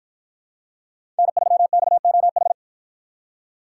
Morse code, heard as a keyed tone, about 53 wpm, 700 Hz. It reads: N3XOH